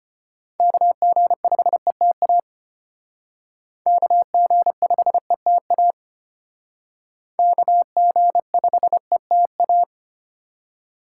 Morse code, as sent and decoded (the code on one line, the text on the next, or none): -.- --. ..... - .- / -.- --. ..... - .- / -.- --. ..... - .-
KG5ETA KG5ETA KG5ETA